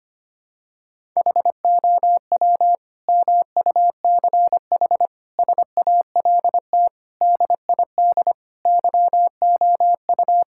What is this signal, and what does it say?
Morse code, heard as a keyed tone, about 25 wpm, 700 Hz. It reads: HOW MUCH SALT DID YOU
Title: Sentences from Top 1000 Words - Review Part 5 - 25wpm